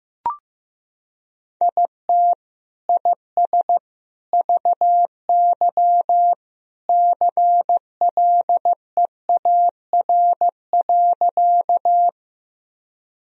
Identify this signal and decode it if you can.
Morse code, heard as a keyed tone, about 16 wpm, 700 Hz. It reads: IT IS VY CLEAR.
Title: Sentences from Top 500 Words and Abbreviations 15wpm